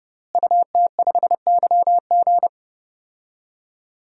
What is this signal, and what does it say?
Morse code, heard as a keyed tone, about 31 wpm, 700 Hz. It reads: UT5YG
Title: DXCC-100 Call Signs - 5 Characters - Repeated 3x - 30wpm